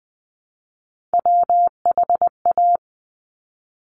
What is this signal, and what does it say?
Morse code, heard as a keyed tone, about 20 wpm, 700 Hz. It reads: WHA
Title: Top Three Letter Combinations in Words 20wpm